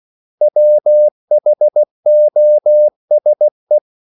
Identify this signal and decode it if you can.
Morse code, heard as a keyed tone, about 16 wpm, 600 Hz. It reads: WHOSE
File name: CW / Practice / 500MostCommonEnglishWords16